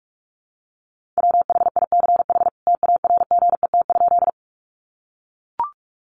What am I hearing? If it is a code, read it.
Morse code, heard as a keyed tone, about 44 wpm, 700 Hz. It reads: WHICH TARGET?
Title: Sentences from Top 1200 Words 45wpm